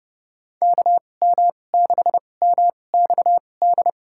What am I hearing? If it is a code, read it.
Morse code, heard as a keyed tone, about 32 wpm, spaced as for 22 wpm, 700 Hz. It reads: KM6MXD
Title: USA Call Signs - 6 Characters - 2x Character Spacing - 30wpm